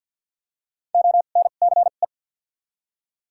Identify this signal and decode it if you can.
Morse code, heard as a keyed tone, about 53 wpm, 700 Hz. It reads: O N C E
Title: 2x Character Spacing - Top 300 Words - 50wpm